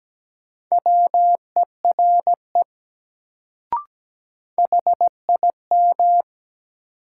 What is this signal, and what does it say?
Morse code, heard as a keyed tone, about 17 wpm, 700 Hz. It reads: WERE HIM
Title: Sets of 2 Words - New Word Drill - 17wpm